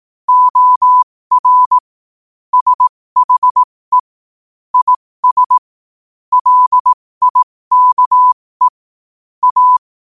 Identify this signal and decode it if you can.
Morse code, heard as a keyed tone, about 18 wpm, spaced as for 15 wpm, 1000 Hz. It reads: OR SHE IS LIKE A
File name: qotdfiles / 15wpm